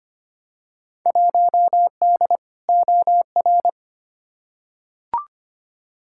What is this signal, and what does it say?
Morse code, heard as a keyed tone, about 25 wpm, 700 Hz. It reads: 1D OR
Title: Field Day Exchanges - 25wpm